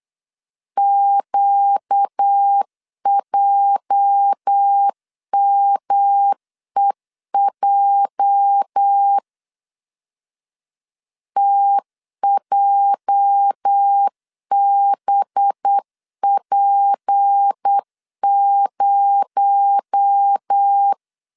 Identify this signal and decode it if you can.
Morse code, heard as a keyed tone, about 8 wpm, 800 Hz. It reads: QJMEJ TJBP0
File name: LEZIONI / 8 YJ45 / 8 YJ 8wpm 3313